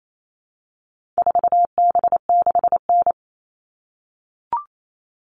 Morse code, heard as a keyed tone, about 28 wpm, 700 Hz. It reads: 4B6N
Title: All Cut Numbers - 4-Digit 28wpm